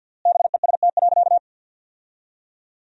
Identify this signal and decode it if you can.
Morse code, heard as a keyed tone, about 53 wpm, 700 Hz. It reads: BEST.
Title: 3x Word-Spacing - Sentences from Top 500 Words - 50wpm